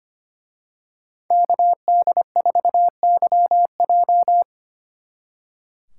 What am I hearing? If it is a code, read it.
Morse code, heard as a keyed tone, about 28 wpm, 700 Hz. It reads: KD4YJ
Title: Speed Racing - USA 2x2 Call Signs 25wpm